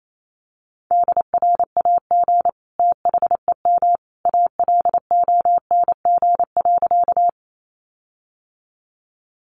Morse code, heard as a keyed tone, about 28 wpm, 700 Hz. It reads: DRAG THEM ALONG.
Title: Sentences from Top 1500 Words 28wpm